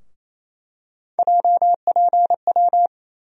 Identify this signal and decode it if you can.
Morse code, heard as a keyed tone, about 28 wpm, 700 Hz. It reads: JPW
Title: USA Call Sign Postfixes - 3 Letters 28wpm